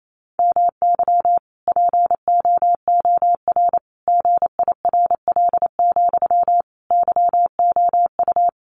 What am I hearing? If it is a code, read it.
Morse code, heard as a keyed tone, about 28 wpm, 700 Hz. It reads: MY POOR GIRL, YOU
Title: Sentences from Top 1000 Words 28wpm